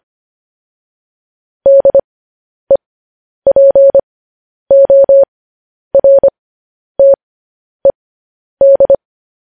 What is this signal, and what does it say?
Morse code, heard as a keyed tone, about 25 wpm, spaced as for 10 wpm, 550 Hz. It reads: DEPORTED
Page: News Headlines in Morse Code at 10 WPM